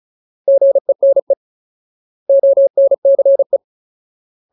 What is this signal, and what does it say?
Morse code, heard as a keyed tone, about 34 wpm, 550 Hz. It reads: GENE ONCE
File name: Ultimate Word List / 35WPM Ultimate